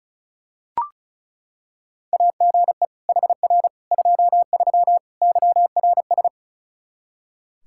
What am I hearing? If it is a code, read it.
Morse code, heard as a keyed tone, about 36 wpm, 700 Hz. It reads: AGE HR 23 YRS
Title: Practice QSO - Exchange 3 - More Info - 35wpm